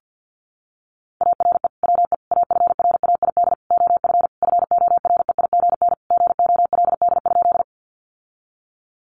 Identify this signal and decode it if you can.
Morse code, heard as a keyed tone, about 50 wpm, 700 Hz. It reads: ARE WE AFRAID OF FOREIGN GOLD?
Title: Sentences from Top 1500 Words 50wpm